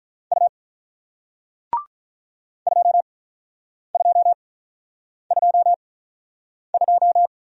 Morse code, heard as a keyed tone, about 47 wpm, spaced as for 9 wpm, 700 Hz. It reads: U 2222